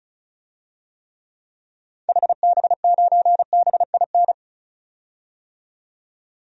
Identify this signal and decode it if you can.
Morse code, heard as a keyed tone, about 36 wpm, 700 Hz. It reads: HB9BIN
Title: DXCC-100 Call Signs - 6 Characters - 35wpm